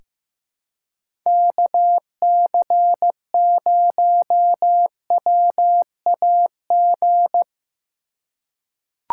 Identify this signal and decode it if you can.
Morse code, heard as a keyed tone, about 15 wpm, 700 Hz. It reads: KC0WAG
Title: USA Call Signs - 3x Character Spacing - 15wpm